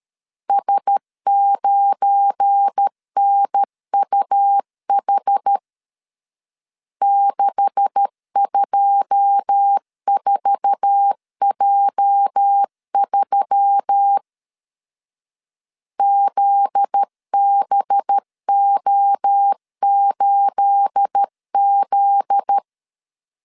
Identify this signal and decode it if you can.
Morse code, heard as a keyed tone, about 13 wpm, 800 Hz. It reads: S9NUH 624J3 ZBO8Z
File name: CW / DET / 1 9 1500C S B 12wpm 3819